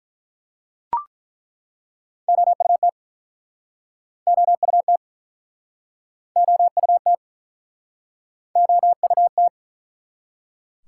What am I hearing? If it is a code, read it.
Morse code, heard as a keyed tone, about 48 wpm, 700 Hz. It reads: OUT OUT OUT OUT